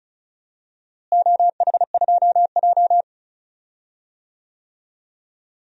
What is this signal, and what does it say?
Morse code, heard as a keyed tone, about 34 wpm, 700 Hz. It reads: OH2J
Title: DXCC-100 Call Signs - 4 Characters - Repeated 3x - 35wpm